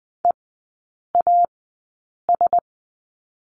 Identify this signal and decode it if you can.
Morse code, heard as a keyed tone, about 20 wpm, spaced as for 8 wpm, 700 Hz. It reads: EAS
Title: 035 Lesson 7e - Letter s - Three Characters 20wpm